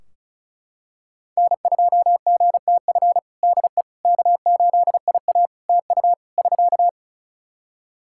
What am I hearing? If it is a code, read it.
Morse code, heard as a keyed tone, about 35 wpm, 700 Hz. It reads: N2GTF DE K8IA TU <SK>
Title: Practice QSO - Exchange 4 - Ending - 35wpm